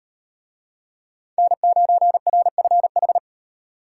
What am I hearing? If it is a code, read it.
Morse code, heard as a keyed tone, about 38 wpm, 700 Hz. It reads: N9RFH